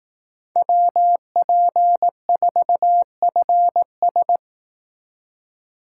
Morse code, heard as a keyed tone, about 18 wpm, 700 Hz. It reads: WP4FS
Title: Speed Racing - USA 2x2 Call Signs 15wpm